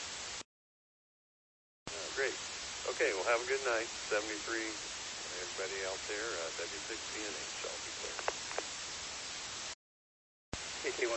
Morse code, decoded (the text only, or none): none